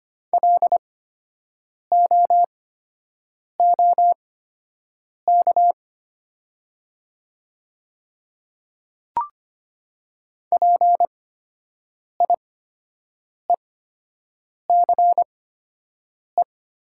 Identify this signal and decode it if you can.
Morse code, heard as a keyed tone, about 25 wpm, spaced as for 7 wpm, 700 Hz. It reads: LOOK PIECE
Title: 8x Character Spacing - Top 500 Words - 25wpm